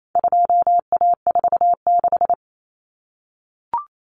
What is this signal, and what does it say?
Morse code, heard as a keyed tone, about 28 wpm, 700 Hz. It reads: 2A46
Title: Common Cut Numbers - 4-Digit 28wpm